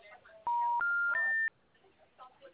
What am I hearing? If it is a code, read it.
Special information tone, with speech.